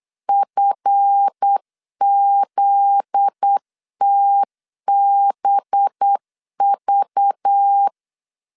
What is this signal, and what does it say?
Morse code, heard as a keyed tone, about 8 wpm, 800 Hz. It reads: FZTBV
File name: LEZIONI / 7 GF9Z / 7 GF9Z 8wpm 3213